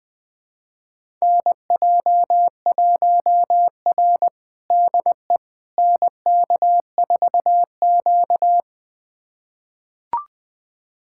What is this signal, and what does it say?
Morse code, heard as a keyed tone, about 20 wpm, 700 Hz. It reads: NJ1R DE NK4Q
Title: USA Callsign DE Callsign - 4 Characters - 20wpm